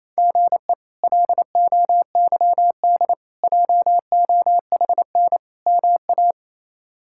Morse code, heard as a keyed tone, about 28 wpm, 700 Hz. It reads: GE LOYD JOHN MA
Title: K1USN SST QSO Practice - 28wpm